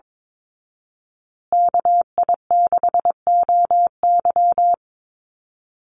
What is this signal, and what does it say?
Morse code, heard as a keyed tone, about 22 wpm, 700 Hz. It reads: KI6OY
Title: Call Signs Repeated 2x - 22wpm